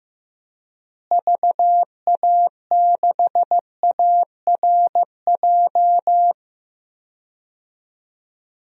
Morse code, heard as a keyed tone, about 15 wpm, 700 Hz. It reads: VA6ARJ
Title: DXCC-100 Call Signs - 6 Characters - Repeated 2x - 15wpm